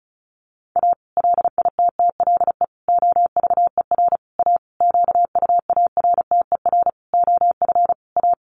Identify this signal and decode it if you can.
Morse code, heard as a keyed tone, about 34 wpm, 700 Hz. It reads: A LITTLE OVER A QUARTER OF A
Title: Sentences from Top 1300 Words 35wpm